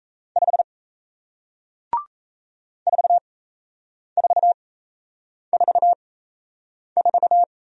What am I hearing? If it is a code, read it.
Morse code, heard as a keyed tone, about 38 wpm, spaced as for 8 wpm, 700 Hz. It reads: 5 4444